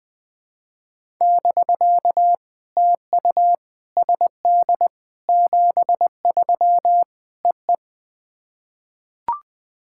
Morse code, heard as a keyed tone, about 20 wpm, 700 Hz. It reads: <BK> TU SD 73 EE
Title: POTA QSO Practice - 20wpm